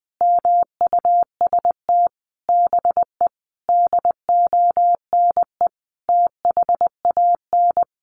Morse code, heard as a keyed tone, about 20 wpm, 700 Hz. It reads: MUST BE DONE THAN
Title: Sentences from Top 1400 Words 20wpm